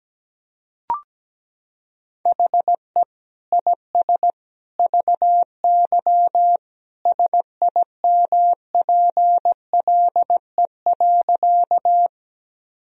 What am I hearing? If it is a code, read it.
Morse code, heard as a keyed tone, about 17 wpm, 700 Hz. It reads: HE IS VY SIMPLE.
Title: Sentences from Top 400 Words and Abbreviations 17wpm